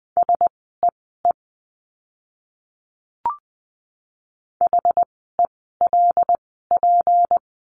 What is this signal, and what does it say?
Morse code, heard as a keyed tone, about 20 wpm, spaced as for 15 wpm, 700 Hz. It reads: SEE HELP
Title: 206 Lesson 33f - Letter j - Words 20wpm